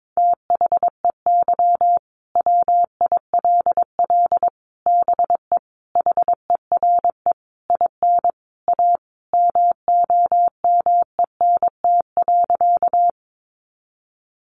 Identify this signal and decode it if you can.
Morse code, heard as a keyed tone, about 22 wpm, 700 Hz. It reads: THEY WILL BE HERE IN A MOMENT.
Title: Sentences from Top 1000 Words 22wpm